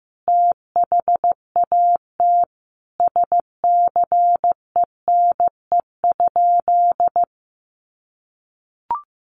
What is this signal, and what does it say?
Morse code, heard as a keyed tone, about 15 wpm, 700 Hz. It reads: THAT SCENE?
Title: Sentences from Top 1100 Words 15wpm